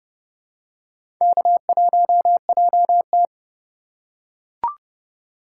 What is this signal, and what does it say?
Morse code, heard as a keyed tone, about 30 wpm, 700 Hz. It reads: K1JT
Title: USA Call Signs - 4 Characters - 8x Character Spacing - 30wpm